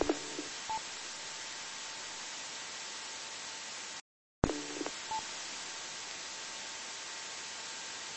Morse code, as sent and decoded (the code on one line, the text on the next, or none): none